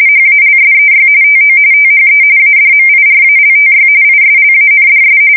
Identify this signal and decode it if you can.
RTTY, about 45.45 baud. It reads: HALSEY TO THE ADMIRALTY ISLANDS
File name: fsk170 / USS-New-Jersey